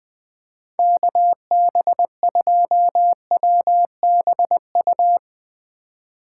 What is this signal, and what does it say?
Morse code, heard as a keyed tone, about 19 wpm, 700 Hz. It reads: KB2WBU
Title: Speed Racing - USA Call Signs 20wpm